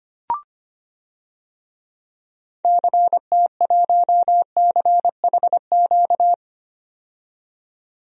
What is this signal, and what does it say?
Morse code, heard as a keyed tone, about 25 wpm, 700 Hz. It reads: CT1CHQ
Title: DXCC-100 Call Signs - 6 Characters - 25wpm